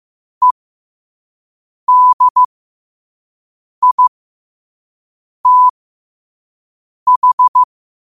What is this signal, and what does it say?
Morse code, heard as a keyed tone, about 15 wpm, spaced as for 5 wpm, 1000 Hz. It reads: EDITH
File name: qotdfiles / 5wpm